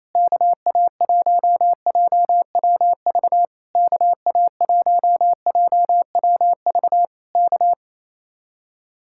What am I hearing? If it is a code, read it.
Morse code, heard as a keyed tone, about 26 wpm, 700 Hz. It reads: KA1JWV KA1JWV K
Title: Speed-Racing Common QSO Phrases - 28wpm